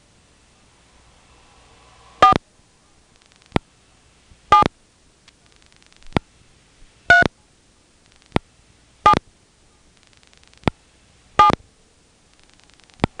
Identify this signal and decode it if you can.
DTMF, heard as telephone touch tones, tones of about 110 ms. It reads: 4 7 3 7 7